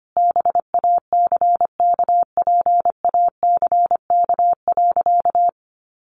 Morse code, heard as a keyed tone, about 25 wpm, 700 Hz. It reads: BACKPACK.
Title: Additional Easy Phrases 25wpm